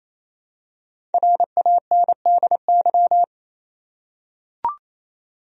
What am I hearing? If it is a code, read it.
Morse code, heard as a keyed tone, about 28 wpm, 700 Hz. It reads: RANDY
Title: USA Common Names - Max 5 Letters - 28wpm